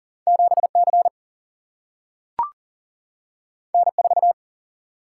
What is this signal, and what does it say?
Morse code, heard as a keyed tone, about 40 wpm, 700 Hz. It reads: 7C N4
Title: Single-Letter Single-Number 40wpm